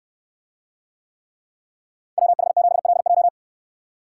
Morse code, heard as a keyed tone, about 68 wpm, 700 Hz. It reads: WH7LJ